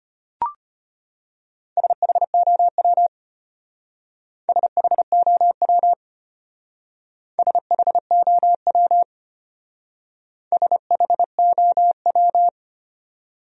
Morse code, heard as a keyed tone, about 30 wpm, 700 Hz. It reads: SHOW SHOW SHOW SHOW
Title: Speed-Racing Top 300 Words 25wpm